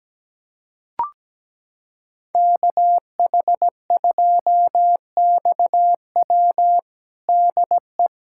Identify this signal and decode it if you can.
Morse code, heard as a keyed tone, about 17 wpm, 700 Hz. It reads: KH2XW DE
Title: USA Callsign DE Callsign 17wpm